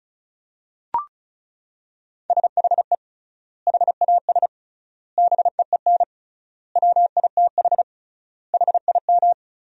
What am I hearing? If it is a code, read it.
Morse code, heard as a keyed tone, about 35 wpm, 700 Hz. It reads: SHE HAS BEEN WITH HIM
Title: 3x Word-Spacing - Sentences from Top 200 Words - 35wpm